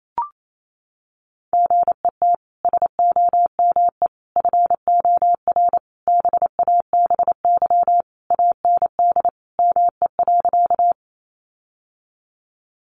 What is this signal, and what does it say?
Morse code, heard as a keyed tone, about 28 wpm, 700 Hz. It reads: GET SOME FOR BABY AND ME.
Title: Sentences from Top 1300 Words 28wpm